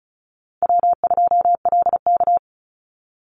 Morse code, heard as a keyed tone, about 34 wpm, 700 Hz. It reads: W2LK
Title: Call Signs Repeated 2x - 35wpm